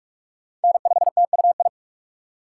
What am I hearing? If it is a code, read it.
Morse code, heard as a keyed tone, about 42 wpm, 700 Hz. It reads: N5TUI
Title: USA Call Signs - 5 Characters - 3x Character Spacing - 45wpm